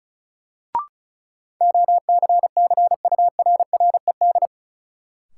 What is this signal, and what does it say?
Morse code, heard as a keyed tone, about 36 wpm, 700 Hz. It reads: OCCURRED
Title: Sets of 5 Words Encore - New Word Drill - 35wpm